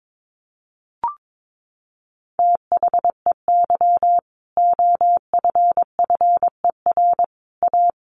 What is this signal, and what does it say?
Morse code, heard as a keyed tone, about 22 wpm, 700 Hz. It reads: THEY OFFER A
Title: Sentences from Top 900 Words 22wpm